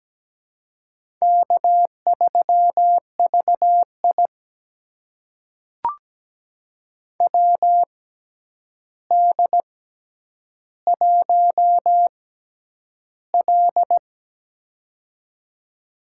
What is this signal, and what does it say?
Morse code, heard as a keyed tone, about 17 wpm, spaced as for 9 wpm, 700 Hz. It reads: K3VI WD1L
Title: USA Call Signs - 4 Characters - 6x Character Spacing - 17wpm